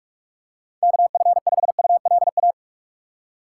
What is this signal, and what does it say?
Morse code, heard as a keyed tone, about 48 wpm, 700 Hz. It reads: KF5ULA
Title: USA Call Signs - 6 Characters - 3x Character Spacing - 45wpm